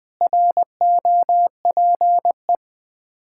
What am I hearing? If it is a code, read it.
Morse code, heard as a keyed tone, about 20 wpm, 700 Hz. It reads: ROPE
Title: Difficult Words with Hints - 20wpm